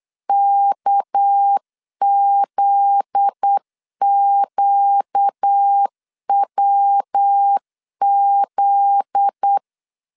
Morse code, heard as a keyed tone, about 8 wpm, 800 Hz. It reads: KZQWZ
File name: LEZIONI / 7 GF9Z / 7 GF9Z 8wpm 3213